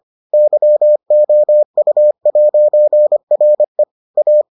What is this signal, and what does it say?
Morse code, heard as a keyed tone, about 25 wpm, 600 Hz. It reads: YOU'RE A